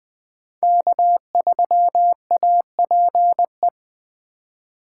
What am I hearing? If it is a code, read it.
Morse code, heard as a keyed tone, about 21 wpm, 700 Hz. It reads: K3APE
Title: USA Call Signs - 3x Character Spacing - 20wpm